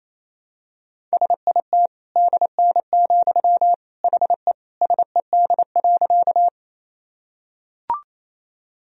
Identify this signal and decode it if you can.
Morse code, heard as a keyed tone, about 28 wpm, 700 Hz. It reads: SIT DN, HE SED.
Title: Sentences from Top 500 Words and Abbreviations 28wpm